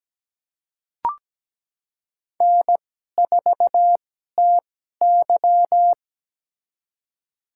Morse code, heard as a keyed tone, about 17 wpm, spaced as for 12 wpm, 700 Hz. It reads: N4TY
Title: USA Call Signs - 4 Characters - 2x Character Spacing - 17wpm